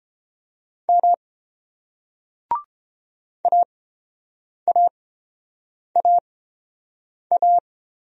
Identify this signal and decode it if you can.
Morse code, heard as a keyed tone, about 29 wpm, spaced as for 7 wpm, 700 Hz. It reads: M AAAA